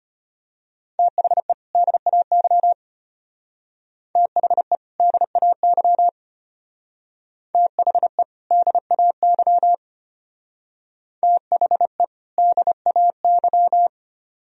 Morse code, heard as a keyed tone, about 30 wpm, 700 Hz. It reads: THE DAY THE DAY THE DAY THE DAY
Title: Speed-Racing - Sets of 2 Words 25wpm